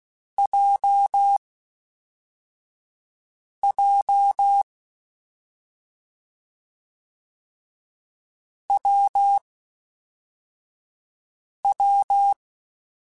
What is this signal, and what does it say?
Morse code, heard as a keyed tone, about 16 wpm, 800 Hz. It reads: J J W W
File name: Difficult Combos / Difficult Combos-mp3s / WPJ WPJ1 @5